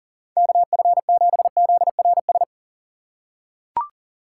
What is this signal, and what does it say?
Morse code, heard as a keyed tone, about 40 wpm, 700 Hz. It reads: KF7ZRS
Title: USA Call Signs - 6 Characters - 2x Character Spacing - 40wpm